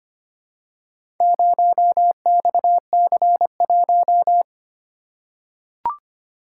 Morse code, heard as a keyed tone, about 25 wpm, 700 Hz. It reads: 0XC1
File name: practice / 25wpm / Four Letter-Number Combinations - 25wpm